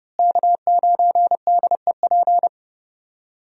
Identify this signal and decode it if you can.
Morse code, heard as a keyed tone, about 32 wpm, 700 Hz. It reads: K9DEP
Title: USA Call Signs - 5 Characters - Repeated 3x - 30wpm